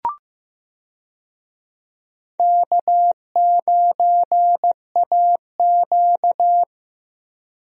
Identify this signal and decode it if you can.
Morse code, heard as a keyed tone, about 15 wpm, 700 Hz. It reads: K9AQ
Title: USA Call Signs - 4 Characters - Repeated 2x - 15wpm